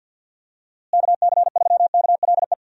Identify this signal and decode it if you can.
Morse code, heard as a keyed tone, about 52 wpm, 700 Hz. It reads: KC3KLE
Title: USA Call Signs - 6 Characters - Repeated 3x - 50wpm